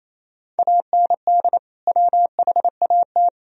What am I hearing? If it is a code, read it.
Morse code, heard as a keyed tone, about 28 wpm, 700 Hz. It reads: AND WHAT
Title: Sentences from Top 1000 Words - Review Part 4 - 28wpm